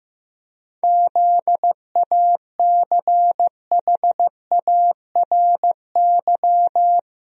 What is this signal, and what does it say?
Morse code, heard as a keyed tone, about 15 wpm, 700 Hz. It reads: ZACHARY
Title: USA Common Names - 15wpm